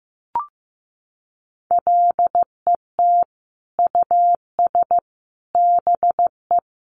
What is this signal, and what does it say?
Morse code, heard as a keyed tone, about 15 wpm, 700 Hz. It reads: LET US BE
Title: Sentences from Top 1300 Words 15wpm